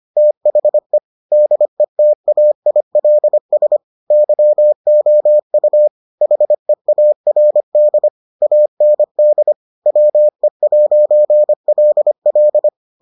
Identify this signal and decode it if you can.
Morse code, heard as a keyed tone, about 25 wpm, 600 Hz. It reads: THE DETAILS YOU HEARD AND WE'LL